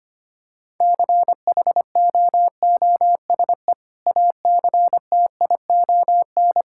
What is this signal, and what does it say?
Morse code, heard as a keyed tone, about 25 wpm, 700 Hz. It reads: CHOOSE ACTION
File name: practice / 25wpm / Sentences with 2-Word Phrases - 25wpm